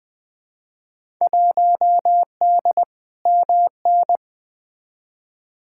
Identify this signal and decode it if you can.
Morse code, heard as a keyed tone, about 20 wpm, 700 Hz. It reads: 1D MN
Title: Field Day Exchanges - 20wpm